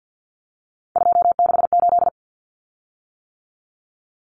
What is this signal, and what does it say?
Morse code, heard as a keyed tone, about 50 wpm, 700 Hz. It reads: 268
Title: Common Cut Numbers - 3-Digit 50wpm